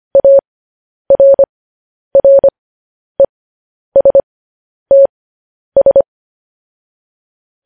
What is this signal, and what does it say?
Morse code, heard as a keyed tone, about 25 wpm, spaced as for 10 wpm, 550 Hz. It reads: ARRESTS